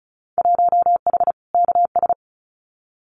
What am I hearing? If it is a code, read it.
Morse code, heard as a keyed tone, about 34 wpm, 700 Hz. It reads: 1H KS
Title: Winter Field Day Exchanges 35wpm